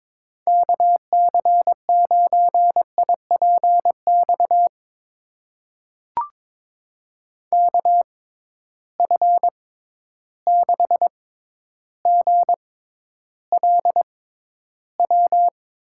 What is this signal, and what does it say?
Morse code, heard as a keyed tone, about 21 wpm, spaced as for 11 wpm, 700 Hz. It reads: KC9IPX KF6GLW
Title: USA Call Signs - 6 Characters - 6x Character Spacing - 22wpm